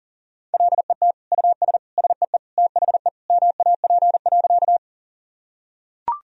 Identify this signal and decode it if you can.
Morse code, heard as a keyed tone, about 40 wpm, 700 Hz. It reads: LET US SEE THE MAP.
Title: Sentences from Top 1000 Words - Review Part 2 - 40wpm